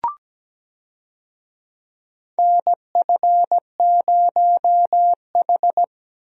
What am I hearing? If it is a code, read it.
Morse code, heard as a keyed tone, about 17 wpm, 700 Hz. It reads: NF0H
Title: USA Call Signs - 4 Characters - Repeated 2x - 17wpm